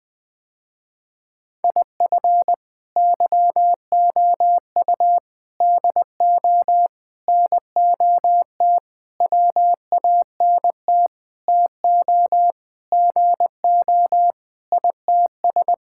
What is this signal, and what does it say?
Morse code, heard as a keyed tone, about 21 wpm, 700 Hz. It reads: IF YOU DO NOT WANT TO GO ITS
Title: Sentences with 5-Word Phrases - Spoken Beginning - 20wpm